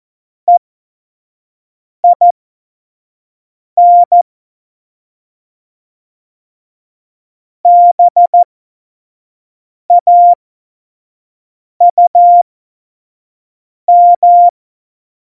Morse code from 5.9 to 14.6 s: BAUM